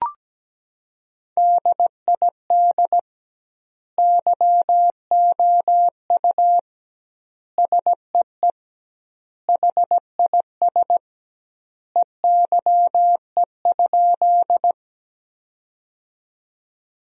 Morse code, heard as a keyed tone, about 17 wpm, 700 Hz. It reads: DID YOU SEE HIS EYE?
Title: 2x Word-Spacing - Sentences from Top 300 Words - 17wpm